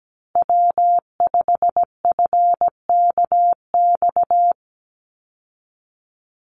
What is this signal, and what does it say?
Morse code, heard as a keyed tone, about 17 wpm, 700 Hz. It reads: W5FKX
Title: Call Signs Repeated 3x - 17wpm